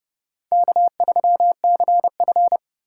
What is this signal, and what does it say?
Morse code, heard as a keyed tone, about 30 wpm, 700 Hz. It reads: K3CF
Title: USA Call Signs - 4 Characters - 4x Character Spacing - 30wpm